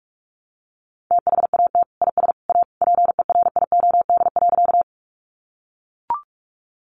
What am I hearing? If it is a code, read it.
Morse code, heard as a keyed tone, about 44 wpm, 700 Hz. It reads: THAT IS A PERIOD.
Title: Sentences from Top 800 Words 45wpm